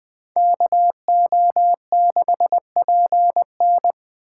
Morse code, heard as a keyed tone, about 21 wpm, 700 Hz. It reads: KO6PN